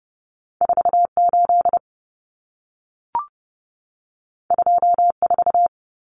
Morse code, heard as a keyed tone, about 30 wpm, 700 Hz. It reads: 48 24
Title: Numbers - 2-Digit 30wpm